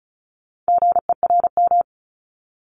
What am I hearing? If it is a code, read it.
Morse code, heard as a keyed tone, about 36 wpm, 700 Hz. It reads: GERM
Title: Multi-letter Phonemes 35wpm